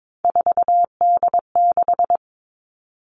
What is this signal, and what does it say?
Morse code, heard as a keyed tone, about 22 wpm, 700 Hz. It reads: 4D6
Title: All Cut Numbers - 3-Digit 22wpm